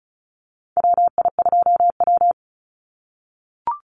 Morse code, heard as a keyed tone, about 36 wpm, 700 Hz. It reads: WI2W
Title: NA SOTA Chaser Callsigns 35wpm